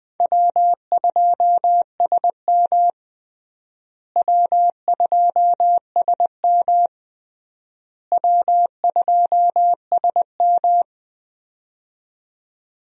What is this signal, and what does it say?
Morse code, heard as a keyed tone, about 20 wpm, 700 Hz. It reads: W2SM W2SM W2SM
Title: USA Call Signs - 4 Characters - Repeated 3x - 20wpm